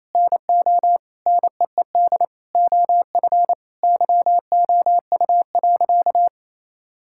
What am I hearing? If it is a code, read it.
Morse code, heard as a keyed tone, about 27 wpm, 700 Hz. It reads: NO NEED OF YOU.
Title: Sentences from Top 1000 Words - Review Part 5 - 28wpm